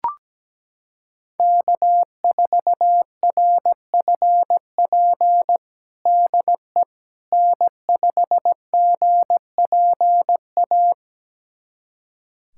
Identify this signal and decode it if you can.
Morse code, heard as a keyed tone, about 17 wpm, 700 Hz. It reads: K4RFP DE N5GPA
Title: USA Callsign DE Callsign - 5 Characters - 17wpm